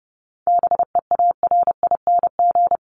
Morse code, heard as a keyed tone, about 30 wpm, 700 Hz. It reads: BEARING
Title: ING Suffix - Top 100 Words with Base Words 30wpm